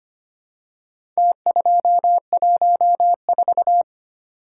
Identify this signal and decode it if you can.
Morse code, heard as a keyed tone, about 25 wpm, 700 Hz. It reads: T214
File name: practice / 25wpm / Four Letter-Number Combinations - 25wpm